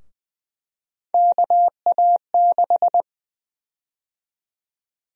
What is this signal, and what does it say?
Morse code, heard as a keyed tone, about 21 wpm, 700 Hz. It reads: KA6
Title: USA Call Sign Prefixes - 20wpm